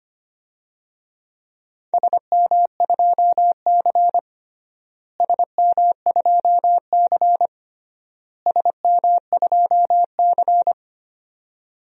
Morse code, heard as a keyed tone, about 25 wpm, 700 Hz. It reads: SM2C SM2C SM2C
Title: DXCC-100 Call Signs - 4 Characters - Repeated 3x - 25wpm